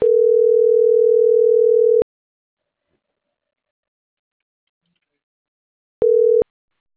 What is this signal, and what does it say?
Ringback tone (North America), 440 + 480 Hz.